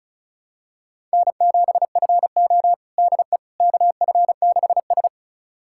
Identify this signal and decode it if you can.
Morse code, heard as a keyed tone, about 36 wpm, 700 Hz. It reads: N7FO DE KF6S
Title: USA Callsign DE Callsign - 4 Characters - 35wpm